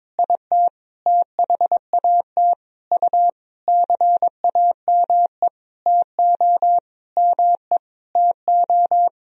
Morse code, heard as a keyed tone, about 22 wpm, 700 Hz. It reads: IT THAT U CAME TO ME TO
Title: Sentences from Top 200 Words and Abbreviations 22wpm